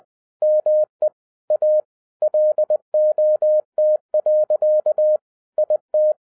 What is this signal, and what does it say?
Morse code, heard as a keyed tone, about 20 wpm, 610 Hz. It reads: ME A LOT. IT
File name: 2012-01-mp3 / n05-CAME BACK TO HAM RADIO AFTER 30 YEARS OF INTERVAL-20wpm